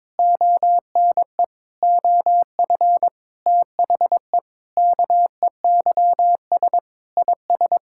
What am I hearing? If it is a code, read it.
Morse code, heard as a keyed tone, about 22 wpm, 700 Hz. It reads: ONE OF THE KEYS IS
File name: practice / 22wpm / Sentences with 4-Word Phrases - 22wpm